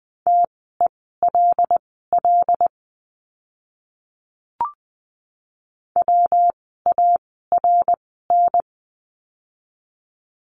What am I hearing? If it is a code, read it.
Morse code, heard as a keyed tone, about 20 wpm, spaced as for 14 wpm, 700 Hz. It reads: TELL WARN
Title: 115 Lesson 19f - Letter w - Words 20wpm